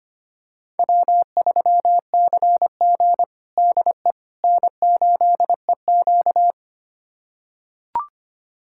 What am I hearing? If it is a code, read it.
Morse code, heard as a keyed tone, about 25 wpm, 700 Hz. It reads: W3CG DE N8EQ
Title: USA Callsign DE Callsign - 4 Characters - 25wpm